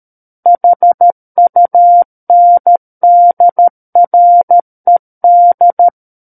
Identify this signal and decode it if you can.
Morse code, heard as a keyed tone, about 13 wpm, 700 Hz. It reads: HUNDRED
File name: Book One / 04-Chapter 4 - 13 WPM